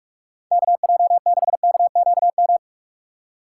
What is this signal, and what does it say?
Morse code, heard as a keyed tone, about 46 wpm, 700 Hz. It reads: KJ6KQM
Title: USA Call Signs - 6 Characters - 4x Character Spacing - 45wpm